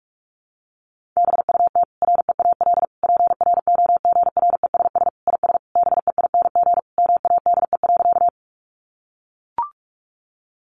Morse code, heard as a keyed tone, about 44 wpm, 700 Hz. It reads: BUT REAL PROGRESS IS BEING MADE.
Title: Sentences from Top 1400 Words 45wpm